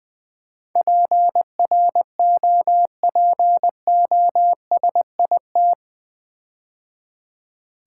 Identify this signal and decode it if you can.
Morse code, heard as a keyed tone, about 20 wpm, 700 Hz. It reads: PROPOSIT
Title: Build A Word - 4 Letters At A Time - 11-Letter Words - 20wpm